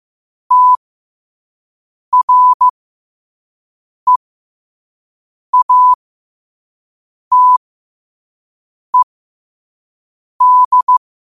Morse code, heard as a keyed tone, about 15 wpm, spaced as for 5 wpm, 1000 Hz. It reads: TREATED
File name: qotdfiles / 5wpm